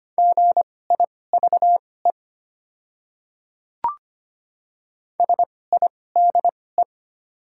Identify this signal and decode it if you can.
Morse code, heard as a keyed tone, about 25 wpm, spaced as for 18 wpm, 700 Hz. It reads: GIVE SIDE
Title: 2x Character Spacing - Top 200 Words - 25wpm